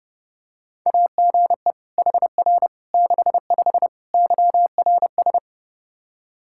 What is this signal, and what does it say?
Morse code, heard as a keyed tone, about 30 wpm, 700 Hz. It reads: AGE HR 65 YRS
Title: Practice QSO - Exchange 3 - More Info - 30wpm